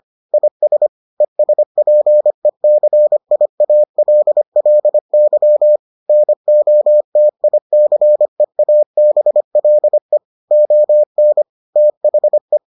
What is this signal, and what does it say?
Morse code, heard as a keyed tone, about 25 wpm, 600 Hz. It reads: IS ESPECIALLY NOTICEABLE ON THE